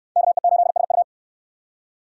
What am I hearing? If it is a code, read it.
Morse code, heard as a keyed tone, about 67 wpm, 700 Hz. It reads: KE8SU